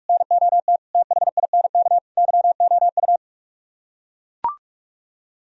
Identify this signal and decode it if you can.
Morse code, heard as a keyed tone, about 44 wpm, 700 Hz. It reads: NOT THINK YOU